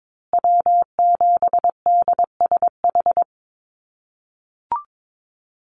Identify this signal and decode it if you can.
Morse code, heard as a keyed tone, about 22 wpm, 700 Hz. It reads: W7DSH
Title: USA Call Signs Repeated 3x 22wpm